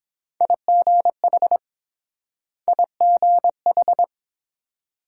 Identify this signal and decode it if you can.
Morse code, heard as a keyed tone, about 30 wpm, 700 Hz. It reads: IGH IGH